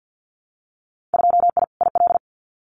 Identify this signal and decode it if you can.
Morse code, heard as a keyed tone, about 50 wpm, 700 Hz. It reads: 2I IL
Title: Winter Field Day Exchanges 50wpm